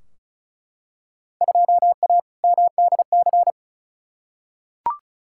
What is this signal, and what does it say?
Morse code, heard as a keyed tone, about 36 wpm, 690 Hz. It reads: 2A MDC
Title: Field Day Exchanges Encore - 35wpm